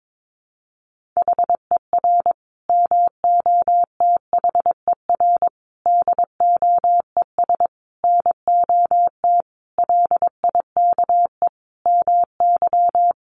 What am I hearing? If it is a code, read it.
Morse code, heard as a keyed tone, about 22 wpm, 700 Hz. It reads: HER MOTHER DOES NOT LIKE MY
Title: Sentences from Top 1400 Words 22wpm